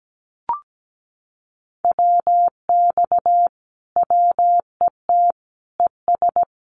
Morse code, heard as a keyed tone, about 17 wpm, 700 Hz. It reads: WX WET ES
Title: Common QSO Phrases Encore 17wpm